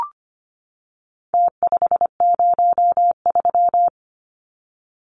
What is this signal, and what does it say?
Morse code, heard as a keyed tone, about 25 wpm, 700 Hz. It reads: T503
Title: Numbers - Contest Style - 4-Digit 25wpm